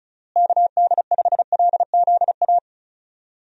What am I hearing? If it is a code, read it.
Morse code, heard as a keyed tone, about 34 wpm, 700 Hz. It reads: KD5LZA
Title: USA Call Signs - 6 Characters - Repeated 3x - 35wpm